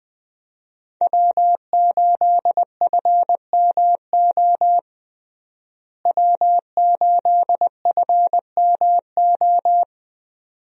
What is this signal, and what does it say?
Morse code, heard as a keyed tone, about 20 wpm, 700 Hz. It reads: W8FMO W8FMO
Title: USA Call Signs - 5 Characters - Repeated 2x - 20wpm